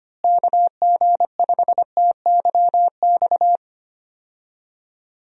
Morse code, heard as a keyed tone, about 25 wpm, 700 Hz. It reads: KG5TYX
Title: USA Call Signs - 6 Characters - Repeated 3x - 25wpm